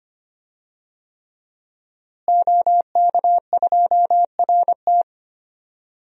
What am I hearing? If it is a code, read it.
Morse code, heard as a keyed tone, about 25 wpm, 700 Hz. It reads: OK2RT
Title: DXCC-100 Call Signs - 5 Characters - Repeated 3x - 25wpm